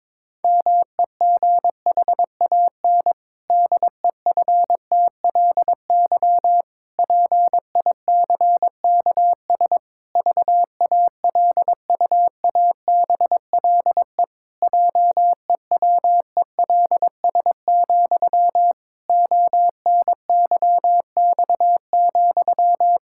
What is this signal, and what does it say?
Morse code, heard as a keyed tone, about 22 wpm, 700 Hz. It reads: MEGHAN DEFTLY PICKS VALUABLE JEWELS, ONYX,
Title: Pangram Challenge v2 - 22wpm